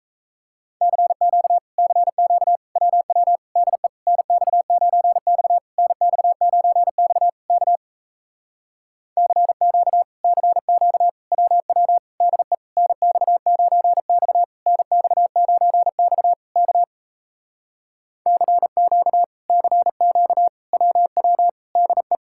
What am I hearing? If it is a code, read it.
Morse code, heard as a keyed tone, about 38 wpm, 700 Hz. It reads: CQ CQ WW DE NX9X NX9X K CQ CQ WW DE NX9X NX9X K CQ CQ WW DE